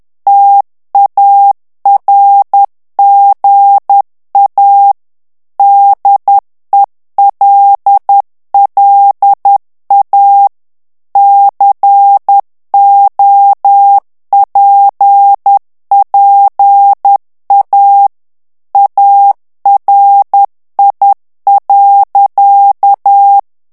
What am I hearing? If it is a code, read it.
Morse code, heard as a keyed tone, about 10 wpm, 800 Hz. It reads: TARGA DELLA COPPA ARI.
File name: ITALIEN texte clair 2 vitesse 10 mots minute